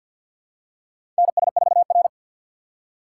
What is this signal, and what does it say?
Morse code, heard as a keyed tone, about 53 wpm, 700 Hz. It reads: NS4R
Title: USA Call Signs - 4 Characters - 6x Character Spacing - 50wpm